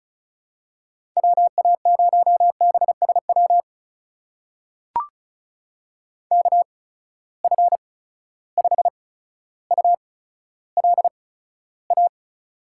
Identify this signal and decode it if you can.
Morse code, heard as a keyed tone, about 36 wpm, spaced as for 15 wpm, 700 Hz. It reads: WA0BSW KF5ULA
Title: USA Call Signs - 6 Characters - 8x Character Spacing - 35wpm